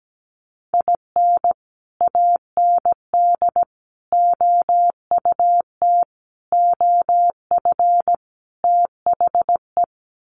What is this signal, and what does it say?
Morse code, heard as a keyed tone, about 17 wpm, 700 Hz. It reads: IN AND OUT OF THE
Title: Sets of 5 Words 17wpm